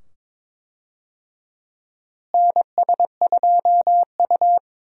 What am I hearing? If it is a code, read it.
Morse code, heard as a keyed tone, about 22 wpm, 700 Hz. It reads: NS2U
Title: USA Call Signs - 4 Characters - Repeated 2x - 22wpm